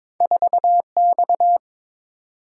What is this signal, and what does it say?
Morse code, heard as a keyed tone, about 31 wpm, spaced as for 27 wpm, 700 Hz. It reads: 4X
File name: Speed-Racing - Two Letter-Number Combinations - 22wpm